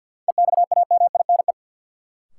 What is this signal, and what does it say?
Morse code, heard as a keyed tone, about 51 wpm, 700 Hz. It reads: EXAMINE
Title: Sentences from Top 1600 Words - New Word Drill Alt - 50wpm